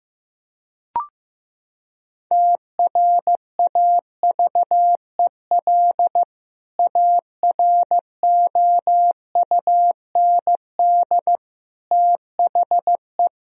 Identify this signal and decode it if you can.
Morse code, heard as a keyed tone, about 16 wpm, 700 Hz. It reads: TRAVEL AROUND THE
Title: Sentences from Top 1000 Words - Review Part 3 - 15wpm